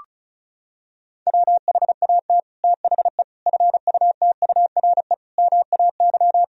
Morse code, heard as a keyed tone, about 36 wpm, 700 Hz. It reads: WHAT THE FUTURE MAY
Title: Sentences from Top 1000 Words - Review Part 2 - 35wpm